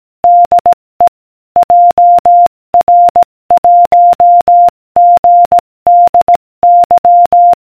Morse code, heard as a keyed tone, about 18 wpm, 700 Hz. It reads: DE JR1GDY